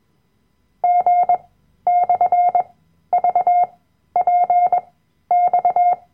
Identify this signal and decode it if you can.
Morse code, heard as a keyed tone, about 21 wpm, spaced as for 12 wpm, 710 Hz. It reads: G/VPX
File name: Workshop Dateien / CW-Online-Kurs / Lektion 010